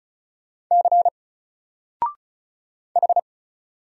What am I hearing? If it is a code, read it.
Morse code, heard as a keyed tone, about 36 wpm, 700 Hz. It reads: C H
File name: Single-Letter-Number-35wpm